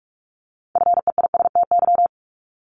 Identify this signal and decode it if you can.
Morse code, heard as a keyed tone, about 46 wpm, 700 Hz. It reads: FEISTY
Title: Multi-letter Phonemes Repeated 3x 45wpm